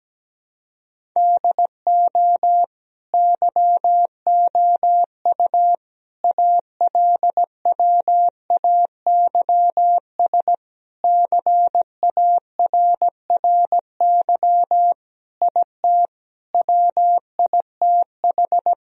Morse code, heard as a keyed tone, about 17 wpm, 700 Hz. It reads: DO YOU ALWAYS CARRY IT WITH
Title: Sentences from Top 1000 Words - Review Part 1 - 17wpm